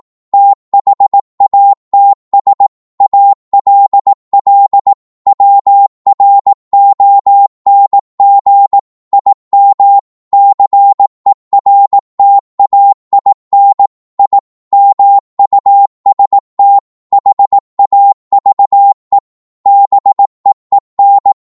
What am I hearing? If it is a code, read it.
Morse code, heard as a keyed tone, about 18 wpm, 800 Hz. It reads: THATS ALL WRONG IM CERTAIN I MUST HAVE BEEN